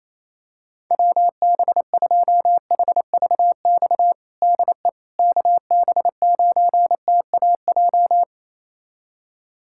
Morse code, heard as a keyed tone, about 28 wpm, 700 Hz. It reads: WB2HVX DE KB9TAJ
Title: USA Callsign DE Callsign - 6 Characters - 28wpm